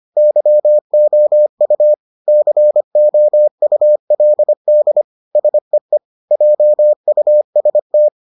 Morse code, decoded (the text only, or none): YOU COULD SEE JUST